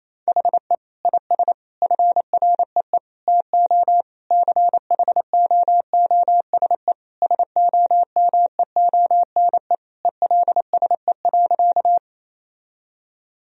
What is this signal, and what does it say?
Morse code, heard as a keyed tone, about 28 wpm, 700 Hz. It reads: HE IS FREE TO CHOOSE SOMEONE ELSE.
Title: Sentences from Top 1000 Words - Review Part 4 - 28wpm